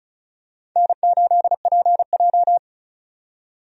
Morse code, heard as a keyed tone, about 34 wpm, 700 Hz. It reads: N8PJ
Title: USA Call Signs - 4 Characters - 35wpm